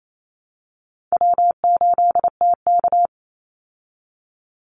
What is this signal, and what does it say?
Morse code, heard as a keyed tone, about 28 wpm, 700 Hz. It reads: W8TK
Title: Easy Call Signs Repeated 2x - 28wpm